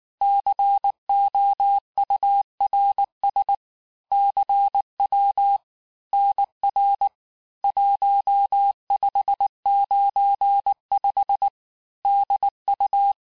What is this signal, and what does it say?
Morse code, heard as a keyed tone, about 19 wpm, 800 Hz. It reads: COURS CW NR 1595 DU
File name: cours-du-lundi files / cours-1595-teste-CHIFFRE-vitesse18-MPM